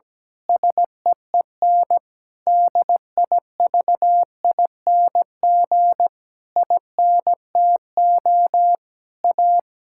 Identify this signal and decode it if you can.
Morse code, heard as a keyed tone, about 17 wpm, 700 Hz. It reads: SEEN DIVING INTO A
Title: Pangram Challenge v2 - 17wpm